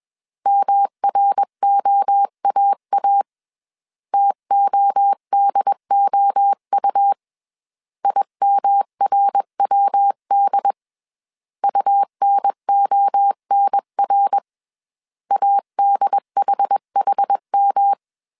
Morse code, heard as a keyed tone, about 21 wpm, 800 Hz. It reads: MROAA TODOU IMRWD UNONR ADHHM